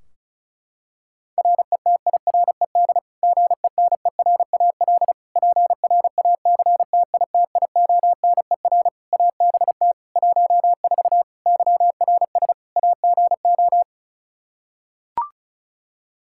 Morse code, heard as a keyed tone, about 36 wpm, 700 Hz. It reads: RETIRED GENERAL PRACTITIONER ABT 14 YRS AGO
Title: Practice QSO - Exchange 3 - More Info - 35wpm